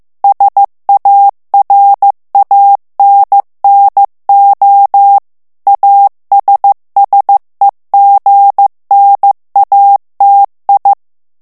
Morse code, heard as a keyed tone, about 15 wpm, 800 Hz. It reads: SARANNO ASSEGNATI
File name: morse mp3 / ITALIEN texte clair 2 vitesse 14 mots minute